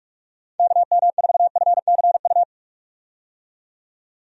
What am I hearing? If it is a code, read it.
Morse code, heard as a keyed tone, about 46 wpm, 700 Hz. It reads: KM4FCU
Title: USA Call Signs - 6 Characters - 45wpm